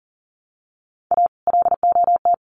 Morse code, heard as a keyed tone, about 37 wpm, 700 Hz. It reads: A LOT
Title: Sets of 5 Words 40wpm